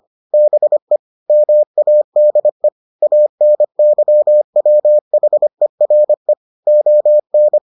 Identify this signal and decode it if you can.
Morse code, heard as a keyed tone, about 25 wpm, 600 Hz. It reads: BE MADE ANYWHERE ON